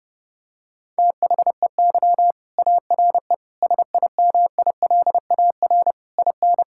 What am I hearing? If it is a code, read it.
Morse code, heard as a keyed tone, about 32 wpm, 700 Hz. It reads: THEY ARE SIMILAR IN
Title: Sentences from Top 1000 Words - Review Part 5 - 30wpm